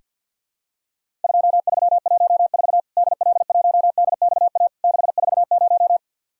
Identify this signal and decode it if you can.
Morse code, heard as a keyed tone, about 50 wpm, 700 Hz. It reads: 2314 DL1DXA 640
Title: Worked All Europe DX Contest - QTC Exchange - 50wpm